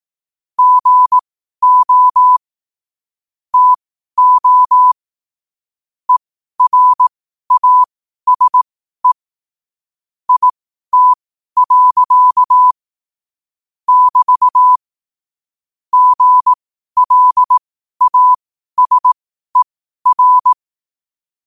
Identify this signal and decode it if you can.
Morse code, heard as a keyed tone, about 18 wpm, spaced as for 12 wpm, 1000 Hz. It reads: GO TO ERASE IT. <BT> GLASER
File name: qotdfiles / 12wpm